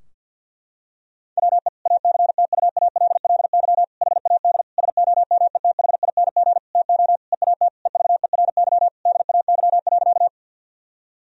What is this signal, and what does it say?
Morse code, heard as a keyed tone, about 52 wpm, 700 Hz. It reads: WE ACTUALLY HAD SOMETHING TO EAT EVERY DAY.
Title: Sentences from Top 1000 Words - Review Part 4 - 50wpm